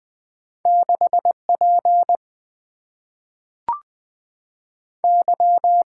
Morse code, heard as a keyed tone, about 20 wpm, 700 Hz. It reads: 6 P Y
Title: Two Letter-Number Combinations - 20wpm